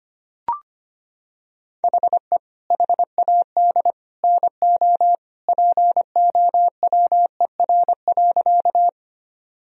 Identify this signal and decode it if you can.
Morse code, heard as a keyed tone, about 25 wpm, 700 Hz. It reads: HE HAD NO POWER.
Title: Sentences from Top 1000 Words - Review Part 2 - 25wpm